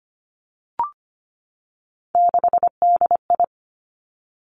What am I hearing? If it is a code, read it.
Morse code, heard as a keyed tone, about 25 wpm, 700 Hz. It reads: 6DI
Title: Three Challenging Letter-Number Combinations 25wpm